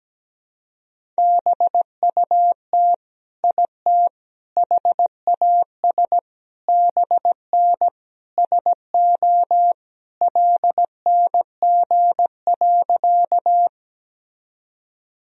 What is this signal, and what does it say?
Morse code, heard as a keyed tone, about 17 wpm, 700 Hz. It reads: BUT IT HAS BN SO LNG.
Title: Sentences from Top 100 Words and Abbreviations 17wpm